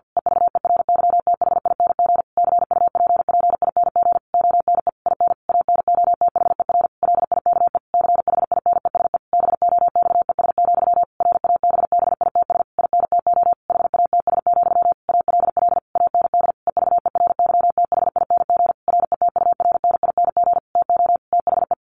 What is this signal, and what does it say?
Morse code, heard as a keyed tone, about 50 wpm, 700 Hz. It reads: EVERYTHING CUPPING ONE IN ANOTHER LIKE CHINESE BOXES, RABBITS INTO HATS, ALL AND EVERYTHING RETURNING TO THE
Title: A Sound Of Thunder 50wpm